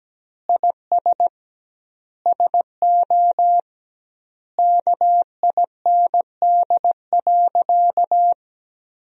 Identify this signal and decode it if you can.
Morse code, heard as a keyed tone, about 17 wpm, 700 Hz. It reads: IS SO KIND.
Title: 2x Word-Spacing - Sentences from Top 500 Words - 17wpm